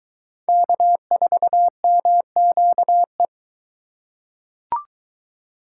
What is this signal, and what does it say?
Morse code, heard as a keyed tone, about 22 wpm, 700 Hz. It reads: K4MQE